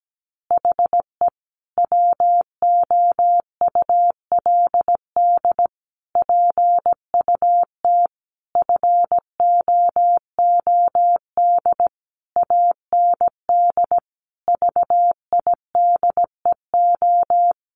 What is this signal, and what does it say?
Morse code, heard as a keyed tone, about 17 wpm, 700 Hz. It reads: HE WOULD PUT FOOD AND VIDEO
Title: Easy Phrases 17wpm